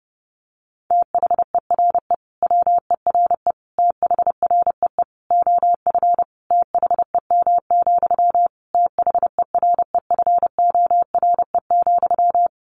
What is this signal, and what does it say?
Morse code, heard as a keyed tone, about 30 wpm, 700 Hz. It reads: THERE WERE THREE OF THEM, THEREFORE,
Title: Sentences from Top 800 Words 30wpm